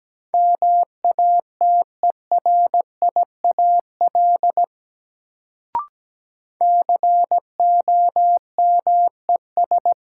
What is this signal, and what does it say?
Morse code, heard as a keyed tone, about 17 wpm, 700 Hz. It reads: MATERIAL COMES
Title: Sentences from Top 1000 Words - Review Part 5 - New Word Drill - 17wpm